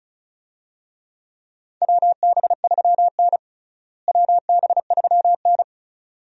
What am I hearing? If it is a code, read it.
Morse code, heard as a keyed tone, about 36 wpm, 700 Hz. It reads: WB3N WB3N
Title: USA Call Signs - 4 Characters - Repeated 2x - 35wpm